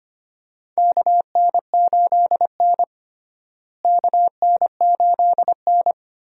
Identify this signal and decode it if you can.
Morse code, heard as a keyed tone, about 25 wpm, 700 Hz. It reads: KN8N KN8N
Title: USA Call Signs - 4 Characters - Repeated 2x - 25wpm